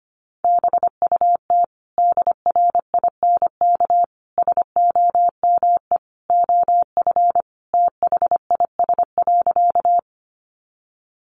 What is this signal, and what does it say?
Morse code, heard as a keyed tone, about 25 wpm, 700 Hz. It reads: BUT DRINK SOME OF THIS.
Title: Sentences from Top 1500 Words 25wpm